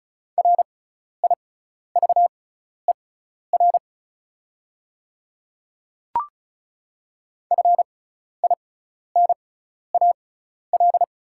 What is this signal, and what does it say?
Morse code, heard as a keyed tone, about 36 wpm, spaced as for 12 wpm, 700 Hz. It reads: RIVER FINAL
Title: 6x Character Spacing - Top 500 Words - 35wpm